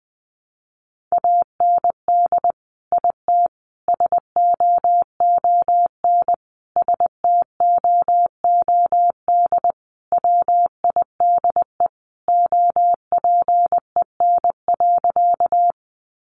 Morse code, heard as a keyed tone, about 20 wpm, 700 Hz. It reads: AND IT SOON STOOD WIDE OPEN.
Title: Sentences from Top 900 Words 20wpm